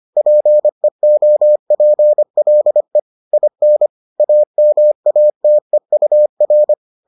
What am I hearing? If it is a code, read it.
Morse code, heard as a keyed tone, about 25 wpm, 600 Hz. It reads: PEOPLE IN AMATEUR